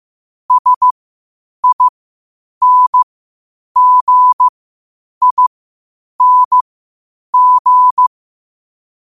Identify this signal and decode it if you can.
Morse code, heard as a keyed tone, about 15 wpm, spaced as for 8 wpm, 1000 Hz. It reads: SINGING
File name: 8wpm